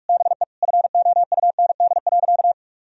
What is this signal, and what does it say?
Morse code, heard as a keyed tone, about 41 wpm, 700 Hz. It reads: BE FOUND.
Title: Sentences from Top 1000 Words - Review Part 3 - 45wpm